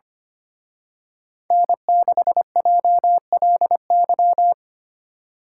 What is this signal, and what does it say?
Morse code, heard as a keyed tone, about 29 wpm, 700 Hz. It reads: N6JLY
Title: Speed Racing - USA Call Signs 25wpm